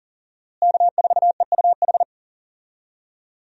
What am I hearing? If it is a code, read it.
Morse code, heard as a keyed tone, about 40 wpm, 700 Hz. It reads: K4EUH
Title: USA Call Signs - 5 Characters - Repeated 3x - 40wpm